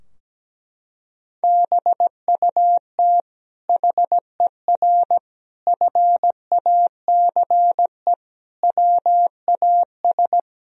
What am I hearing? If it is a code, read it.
Morse code, heard as a keyed tone, about 17 wpm, 700 Hz. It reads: BUT HER FACE WAS
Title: Sentences from Top 1000 Words - Review Part 2 - 17wpm